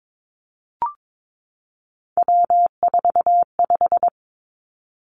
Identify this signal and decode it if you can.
Morse code, heard as a keyed tone, about 22 wpm, 700 Hz. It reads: W45
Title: Three Challenging Letter-Number Combinations 22wpm